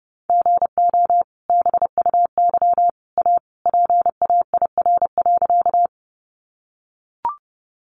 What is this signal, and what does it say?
Morse code, heard as a keyed tone, about 30 wpm, 700 Hz. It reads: GO BUY A PAIR.
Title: Sentences from Top 1200 Words 30wpm